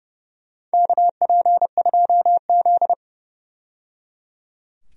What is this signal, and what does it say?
Morse code, heard as a keyed tone, about 29 wpm, 700 Hz. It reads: KP2Z
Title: DXCC-100 Call Signs - 4 Characters - 30wpm